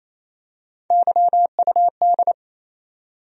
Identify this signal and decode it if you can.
Morse code, heard as a keyed tone, about 28 wpm, 700 Hz. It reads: YUD
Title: USA Call Sign Postfixes - 3 Letters 28wpm